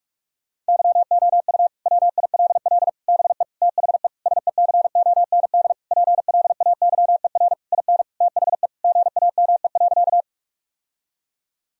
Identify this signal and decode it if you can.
Morse code, heard as a keyed tone, about 46 wpm, 700 Hz. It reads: YOU WILL BE THE SECOND PLAYER IN THE GAME.
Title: Sentences from Top 1000 Words - Review Part 4 - 45wpm